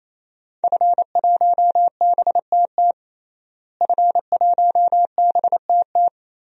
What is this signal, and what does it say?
Morse code, heard as a keyed tone, about 28 wpm, 700 Hz. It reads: F1BTT F1BTT
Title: DXCC-100 Call Signs - 5 Characters - Repeated 3x - 28wpm